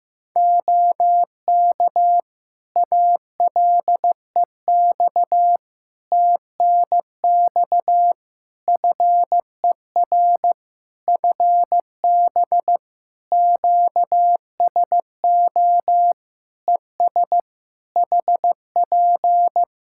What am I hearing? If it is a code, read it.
Morse code, heard as a keyed tone, about 15 wpm, 700 Hz. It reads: OK ALEX TNX FER FB QSO ES HP
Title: Practice QSO - Exchange 4 - Ending - 15wpm